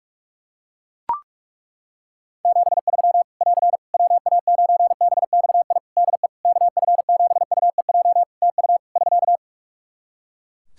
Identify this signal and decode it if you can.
Morse code, heard as a keyed tone, about 41 wpm, 700 Hz. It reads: 73 <AR> WA9BXI DE KF7UEJ TU <SK>
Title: K1USN SST QSO Practice 45wpm